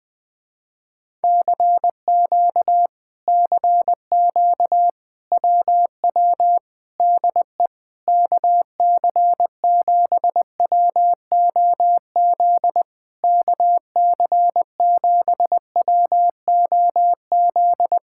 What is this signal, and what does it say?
Morse code, heard as a keyed tone, about 20 wpm, 700 Hz. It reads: CQ CQ WW DE KC7WOZ KC7WOZ
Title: Speed-Racing Common QSO Phrases - 17wpm